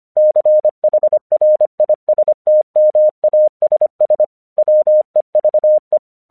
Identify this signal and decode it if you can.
Morse code, heard as a keyed tone, about 25 wpm, 610 Hz. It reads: CHRISTMASS WEVE
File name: 1 English 25wpm / AQC36